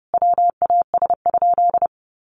Morse code, heard as a keyed tone, about 28 wpm, 700 Hz. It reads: WAS?
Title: Sentences from Top 200 Words 30wpm